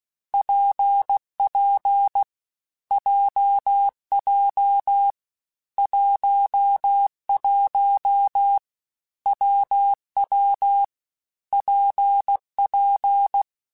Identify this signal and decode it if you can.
Morse code, heard as a keyed tone, about 16 wpm, 800 Hz. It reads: PP JJ 11 WW PP
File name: Difficult Combos / Difficult Combos-mp3s / WPJ WPJ1 @15